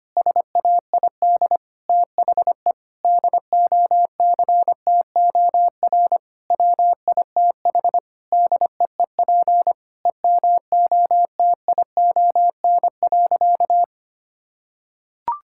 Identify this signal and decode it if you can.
Morse code, heard as a keyed tone, about 25 wpm, 700 Hz. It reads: SAID THE DOCTOR WITH DEEP EMOTION.
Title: Sentences from Top 1600 Words 25wpm